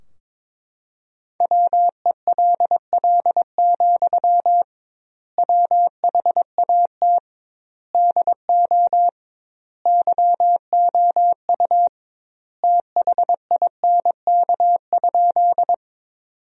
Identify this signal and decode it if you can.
Morse code, heard as a keyed tone, about 22 wpm, 700 Hz. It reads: WELL, WHAT DO YOU THINK?
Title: 2x Word-Spacing - Sentences from Top 200 Words - 22wpm